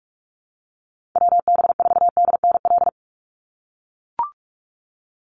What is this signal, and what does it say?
Morse code, heard as a keyed tone, about 44 wpm, 700 Hz. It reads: WB4DNL
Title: Call Signs - 45wpm